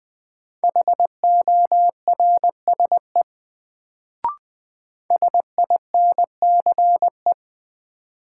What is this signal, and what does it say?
Morse code, heard as a keyed tone, about 20 wpm, 700 Hz. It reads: HORSE SINCE
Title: Sentences from Top 300 Words - New Word Drill - 20wpm